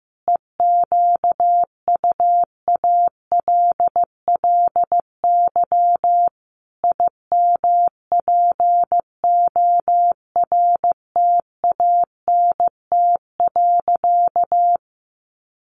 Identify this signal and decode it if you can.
Morse code, heard as a keyed tone, about 15 wpm, 700 Hz. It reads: EQUALLY IMPORTANT.
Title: Sentences from Top 1400 Words 15wpm